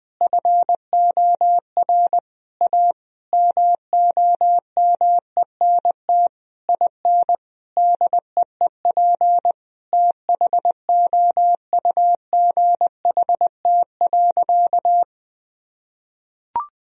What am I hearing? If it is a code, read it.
Morse code, heard as a keyed tone, about 20 wpm, 700 Hz. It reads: FOR A MOMENT IN DEEP THOUGHT.
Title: Sentences from Top 1000 Words - Review Part 3 - 20wpm